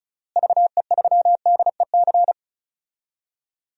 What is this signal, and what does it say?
Morse code, heard as a keyed tone, about 36 wpm, 700 Hz. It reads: VE3DEC
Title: DXCC-100 Call Signs - 6 Characters - Repeated 3x - 35wpm